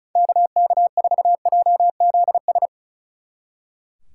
Morse code, heard as a keyed tone, about 44 wpm, 700 Hz. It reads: KK4JZS